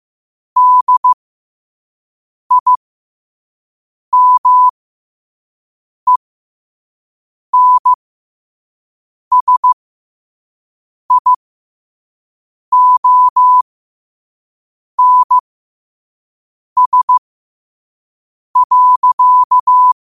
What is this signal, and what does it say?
Morse code, heard as a keyed tone, about 15 wpm, spaced as for 5 wpm, 1000 Hz. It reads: DIMENSIONS.
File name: qotdfiles / 5wpm